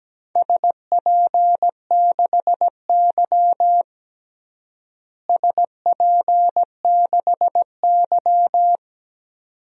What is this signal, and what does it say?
Morse code, heard as a keyed tone, about 17 wpm, 700 Hz. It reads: SP6Y SP6Y
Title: DXCC-100 Call Signs - 4 Characters - Repeated 2x - 17wpm